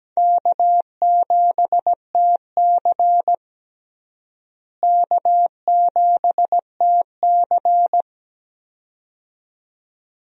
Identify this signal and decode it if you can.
Morse code, heard as a keyed tone, about 17 wpm, 700 Hz. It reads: K7TC K7TC
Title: USA Call Signs - 4 Characters - Repeated 2x - 17wpm